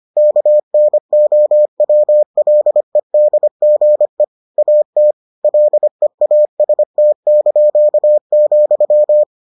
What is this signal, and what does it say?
Morse code, heard as a keyed tone, about 25 wpm, 600 Hz. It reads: KNOWLEDGE AT LEAST),